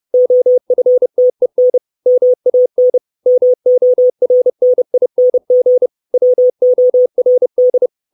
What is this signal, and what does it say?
Morse code, heard as a keyed tone, about 30 wpm, 500 Hz. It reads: OFTEN MAN MORNING WORD
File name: 632 Words 30WPM